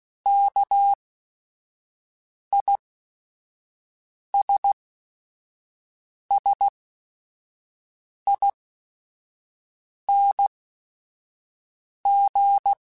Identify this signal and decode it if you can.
Morse code, heard as a keyed tone, about 16 wpm, spaced as for 5 wpm, 800 Hz. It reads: KISSING
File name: Special INGs / Special INGs-mp3s / Sngl Ing @5